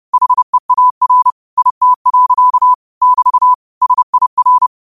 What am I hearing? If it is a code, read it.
Morse code, heard as a keyed tone, about 30 wpm, 1000 Hz. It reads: HEAR IT. <BT> SIR
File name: qotdfiles / 30wpm